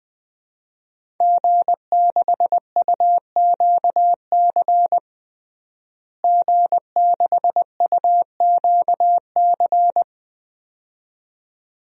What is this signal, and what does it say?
Morse code, heard as a keyed tone, about 21 wpm, 700 Hz. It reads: G6UQC G6UQC
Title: DXCC-100 Call Signs - 5 Characters - Repeated 2x - 20wpm